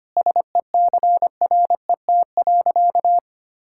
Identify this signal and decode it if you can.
Morse code, heard as a keyed tone, about 25 wpm, 700 Hz. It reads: SECRET.
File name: practice / 25wpm / Sentences with 2-Word Phrases - 25wpm